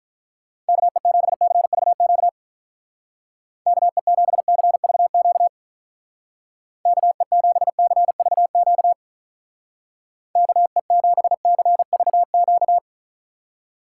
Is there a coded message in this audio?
Yes: Morse code from 0.7 to 12.8 s, heard as a keyed tone, about 44 wpm, 700 Hz. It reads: KE7CVQ KE7CVQ KE7CVQ KE7CVQ